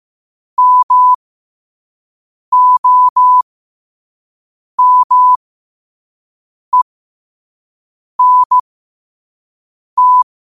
Morse code, heard as a keyed tone, about 15 wpm, spaced as for 5 wpm, 1000 Hz. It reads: MOMENT
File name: qotdfiles / 5wpm